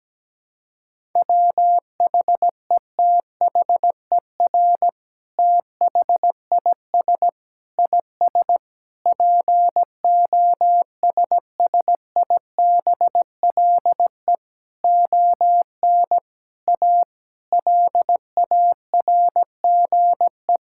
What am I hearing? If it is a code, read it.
Morse code, heard as a keyed tone, about 17 wpm, 700 Hz. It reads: WHETHER THIS IS POSSIBLE ON A LARGE
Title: Sentences from Top 1000 Words - Review Part 5 - 17wpm